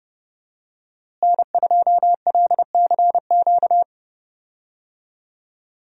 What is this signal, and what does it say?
Morse code, heard as a keyed tone, about 32 wpm, 700 Hz. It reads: N2LCQ
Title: USA Call Signs - 5 Characters - Repeated 2x - 30wpm